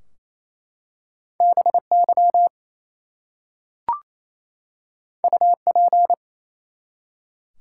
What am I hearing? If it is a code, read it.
Morse code, heard as a keyed tone, about 27 wpm, 700 Hz. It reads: BY UP
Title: Two Letter Words - 28wpm